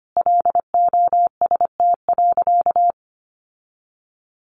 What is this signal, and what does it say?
Morse code, heard as a keyed tone, about 25 wpm, 700 Hz. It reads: LOST.
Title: Sentences from Top 1400 Words 25wpm